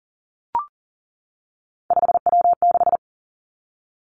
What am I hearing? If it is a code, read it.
Morse code, heard as a keyed tone, about 38 wpm, 700 Hz. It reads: 5W6
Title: Three Challenging Letter-Number Combinations 40wpm